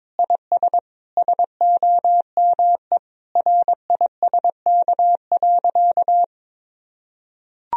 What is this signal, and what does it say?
Morse code, heard as a keyed tone, about 22 wpm, 700 Hz. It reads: IS SOME RISK.
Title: Sentences from Top 1000 Words - Review Part 4 - 22wpm